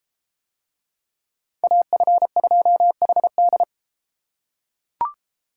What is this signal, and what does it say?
Morse code, heard as a keyed tone, about 31 wpm, 700 Hz. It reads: AF2HD